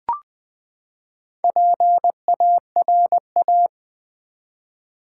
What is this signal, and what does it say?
Morse code, heard as a keyed tone, about 20 wpm, 700 Hz. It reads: PARA
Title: Build A Word - 4 Letters At A Time - 8-Letter Words - 20wpm